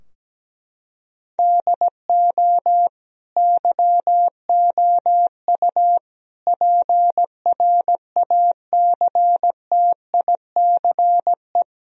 Morse code, heard as a keyed tone, about 17 wpm, 700 Hz. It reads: DO YOU PRACTICE
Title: Sentences from Top 1000 Words - Review Part 4 - 17wpm